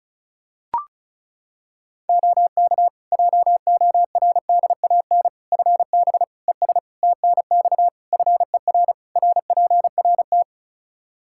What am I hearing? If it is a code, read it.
Morse code, heard as a keyed tone, about 36 wpm, 700 Hz. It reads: OK JORDAN FB ES TNX FER RPRT
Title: Practice QSO - Exchange 2 - Info - 35wpm